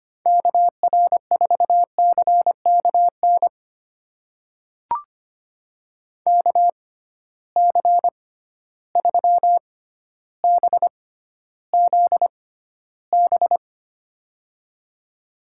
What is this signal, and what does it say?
Morse code, heard as a keyed tone, about 25 wpm, spaced as for 13 wpm, 700 Hz. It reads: KR4CKN KC3BZB
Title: USA Call Signs - 6 Characters - 6x Character Spacing - 25wpm